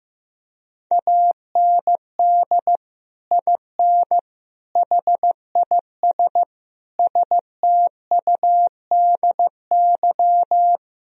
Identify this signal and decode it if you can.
Morse code, heard as a keyed tone, about 15 wpm, 700 Hz. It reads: AND IN HIS STUDY
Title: Sentences from Top 1000 Words - Review Part 1 - 15wpm